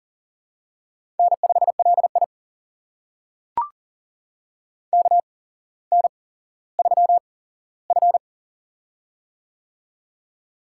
Morse code, heard as a keyed tone, about 38 wpm, spaced as for 17 wpm, 700 Hz. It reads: N5LI KN3F
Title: USA Call Signs - 4 Characters - 8x Character Spacing - 40wpm